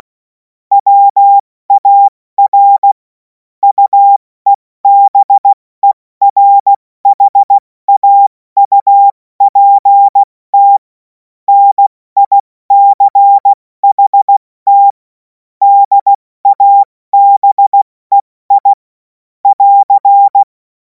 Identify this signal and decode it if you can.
Morse code, heard as a keyed tone, about 16 wpm, 800 Hz. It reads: WAR UEBERHAUPT NICHT DABEI <AR>
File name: Morsekurs / Witze / Witz 10. Februar 2026 70 F80